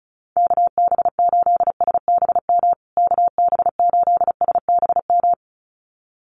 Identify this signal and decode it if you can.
Morse code, heard as a keyed tone, about 34 wpm, 700 Hz. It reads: KB8SBM KB8SBM
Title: USA Call Signs Repeated 2x 35wpm